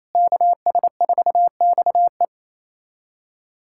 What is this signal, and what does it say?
Morse code, heard as a keyed tone, about 27 wpm, 700 Hz. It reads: KS4XE
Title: USA Callsign DE Callsign 28wpm